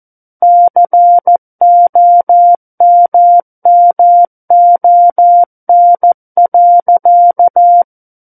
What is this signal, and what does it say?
Morse code, heard as a keyed tone, about 14 wpm, 700 Hz. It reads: COMMON.
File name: Book One / 05-Chapter 5 - 14 WPM